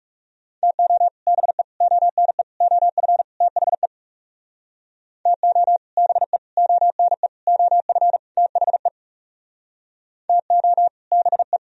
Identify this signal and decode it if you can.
Morse code, heard as a keyed tone, about 40 wpm, 700 Hz. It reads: TO BE ONE OF THE TO BE ONE OF THE TO BE